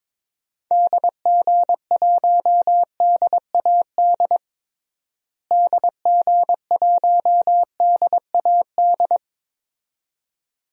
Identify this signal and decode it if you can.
Morse code, heard as a keyed tone, about 22 wpm, 700 Hz. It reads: DG1DAD DG1DAD
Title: DXCC-100 Call Signs - 6 Characters - Repeated 2x - 22wpm